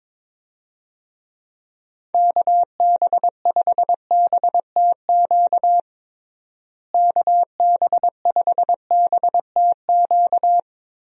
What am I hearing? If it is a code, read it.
Morse code, heard as a keyed tone, about 22 wpm, 700 Hz. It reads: KB5BTQ KB5BTQ
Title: USA Call Signs - 6 Characters - Repeated 3x - 22wpm